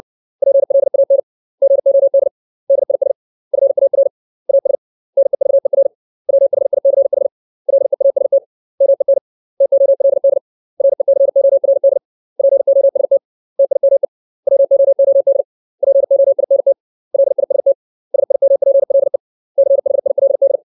Morse code, heard as a keyed tone, about 60 wpm, 550 Hz. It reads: PLAN GOD DIS FAR AS NEVER WHICH LEAST MEN TOLD RECORD POST TIME WOOD POINT LIST SIMPLE CHILD
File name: 632 words / 632 Words 60WPM